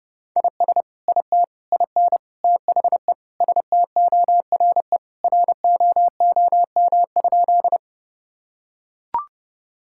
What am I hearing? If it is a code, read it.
Morse code, heard as a keyed tone, about 30 wpm, 700 Hz. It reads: IS IT IN THE STORE ROOM?
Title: Sentences from Top 1000 Words - Review Part 5 - 30wpm